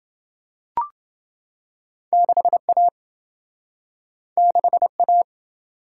Morse code, heard as a keyed tone, about 27 wpm, 700 Hz. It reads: 6A 6A